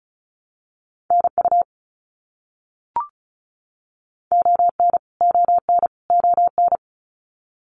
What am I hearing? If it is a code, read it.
Morse code, heard as a keyed tone, about 34 wpm, 700 Hz. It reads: NU ON ON ON
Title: Canadian Provinces and Territories - Repeated 3x - 35wpm